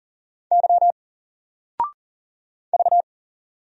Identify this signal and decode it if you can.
Morse code, heard as a keyed tone, about 40 wpm, 700 Hz. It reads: Y V
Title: Single Letter-Number 40wpm